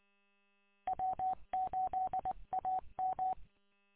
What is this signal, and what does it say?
Morse code, heard as a keyed tone, about 21 wpm, 750 Hz. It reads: W8AM
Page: W8UM 145.230 Ann Arbor